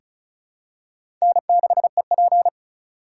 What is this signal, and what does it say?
Morse code, heard as a keyed tone, about 36 wpm, 700 Hz. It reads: N6EP
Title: USA Call Signs - 2x Character Spacing - 35wpm